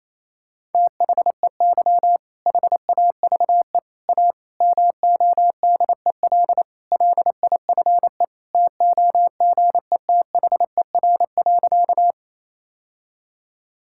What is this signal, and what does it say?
Morse code, heard as a keyed tone, about 28 wpm, 700 Hz. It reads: THEY HAVE A MODEL LIFE TOGETHER.
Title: Sentences from Top 1000 Words - Review Part 5 - 28wpm